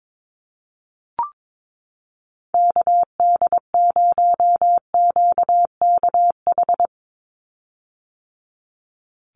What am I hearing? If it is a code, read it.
Morse code, heard as a keyed tone, about 22 wpm, 700 Hz. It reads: KD0QKH
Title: USA Call Signs 22wpm